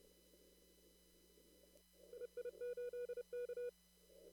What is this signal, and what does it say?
Morse code, heard as a keyed tone, about 30 wpm, 2450 Hz. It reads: EI8K
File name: CW / kt5j wpx cw-op-k5pi